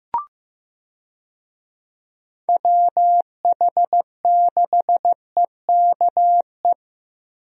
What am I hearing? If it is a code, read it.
Morse code, heard as a keyed tone, about 16 wpm, 700 Hz. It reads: WH6EKE
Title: USA Call Signs - 6 Characters - Repeated 2x - 15wpm